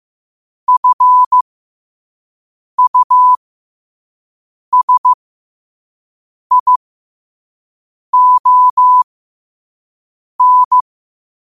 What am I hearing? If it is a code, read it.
Morse code, heard as a keyed tone, about 15 wpm, spaced as for 5 wpm, 1000 Hz. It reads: FUSION
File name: 5wpm